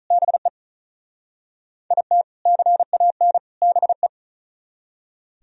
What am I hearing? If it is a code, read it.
Morse code, heard as a keyed tone, about 46 wpm, 700 Hz. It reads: BE IT CAN BE